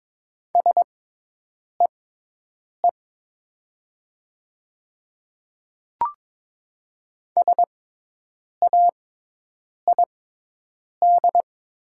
Morse code, heard as a keyed tone, about 22 wpm, spaced as for 8 wpm, 700 Hz. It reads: SEE SAID
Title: 6x Character Spacing - Top 100 Words - 22wpm